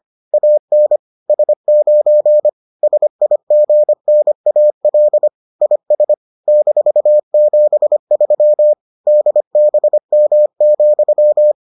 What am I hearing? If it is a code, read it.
Morse code, heard as a keyed tone, about 25 wpm, 600 Hz. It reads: AN S9 SIGNAL IS -73 DBM,